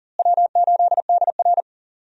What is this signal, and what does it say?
Morse code, heard as a keyed tone, about 40 wpm, 700 Hz. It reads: W8DR
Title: USA Call Signs - 4 Characters - Repeated 2x - 40wpm